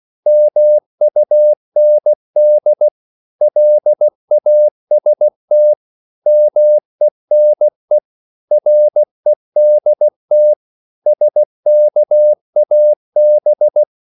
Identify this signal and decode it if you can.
Morse code, heard as a keyed tone, about 16 wpm, 600 Hz. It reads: MUND LAST MENE REDT SKAB
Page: Korte danske ord 16wpm